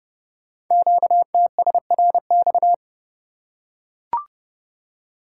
Morse code, heard as a keyed tone, about 30 wpm, 700 Hz. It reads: QTSRX
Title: Code Groups - 5 Letters - 30wpm